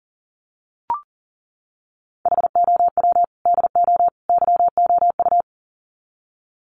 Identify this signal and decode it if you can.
Morse code, heard as a keyed tone, about 40 wpm, 700 Hz. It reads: HOW DO YOU
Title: Sets of 3 Words 40wpm